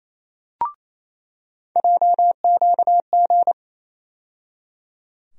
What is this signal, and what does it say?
Morse code, heard as a keyed tone, about 28 wpm, 700 Hz. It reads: JQG
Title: USA Call Sign Postfixes - 3 Letters 28wpm